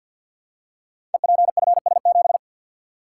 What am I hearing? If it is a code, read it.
Morse code, heard as a keyed tone, about 51 wpm, 700 Hz. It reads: EPFS7
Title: Code Groups - 5 Letter:Numbers - 50wpm